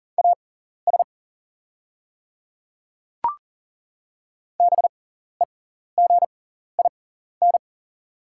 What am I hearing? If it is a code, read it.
Morse code, heard as a keyed tone, about 40 wpm, spaced as for 14 wpm, 700 Hz. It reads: AS BEGIN
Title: 6x Character Spacing - Top 500 Words - 40wpm